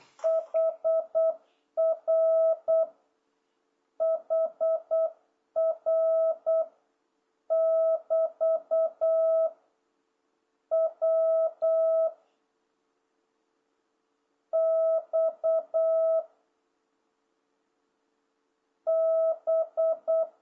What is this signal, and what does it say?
Morse code, heard as a keyed tone, about 8 wpm, 660 Hz. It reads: HR HR <BT> W X B